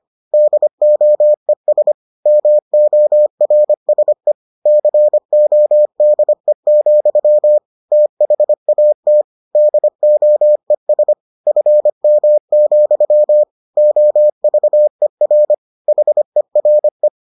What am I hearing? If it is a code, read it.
Morse code, heard as a keyed tone, about 25 wpm, 600 Hz. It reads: DOES MORSE CODE, THAT DOES FM, OVER HERE